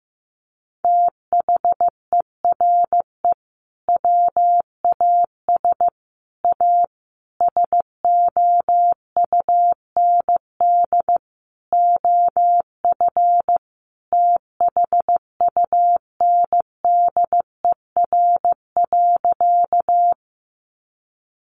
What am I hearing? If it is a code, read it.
Morse code, heard as a keyed tone, about 15 wpm, 700 Hz. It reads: THERE WAS A SOUND OF THUNDER.
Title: A Sound Of Thunder 15wpm